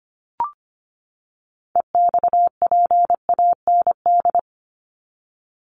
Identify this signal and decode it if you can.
Morse code, heard as a keyed tone, about 25 wpm, 700 Hz. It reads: EXPAND
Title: Top 1251-1500 Words 25wpm